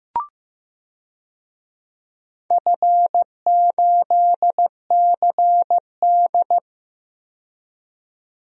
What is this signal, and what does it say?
Morse code, heard as a keyed tone, about 15 wpm, 700 Hz. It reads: F8CD
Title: DXCC-100 Call Signs - 4 Characters - 15wpm